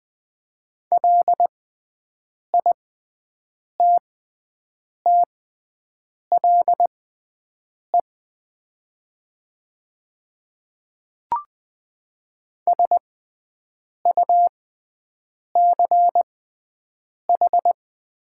Morse code, heard as a keyed tone, about 20 wpm, spaced as for 7 wpm, 700 Hz. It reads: LITTLE SUCH
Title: 6x Character Spacing - Top 300 Words - 20wpm